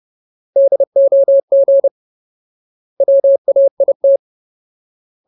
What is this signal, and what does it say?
Morse code, heard as a keyed tone, about 30 wpm, 550 Hz. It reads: DOG WAIT
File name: Ultimate Word List / 30WPM Ultimate